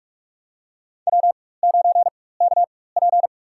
Morse code, heard as a keyed tone, about 46 wpm, spaced as for 21 wpm, 700 Hz. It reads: W9KP
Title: USA Call Signs - 4 Characters - 4x Character Spacing - 45wpm